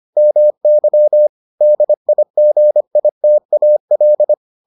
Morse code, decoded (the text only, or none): MY DIGITAL